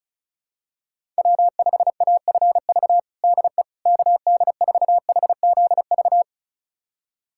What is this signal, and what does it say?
Morse code, heard as a keyed tone, about 36 wpm, 700 Hz. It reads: W5AFV DE KD4HZV
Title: USA Callsign DE Callsign 35wpm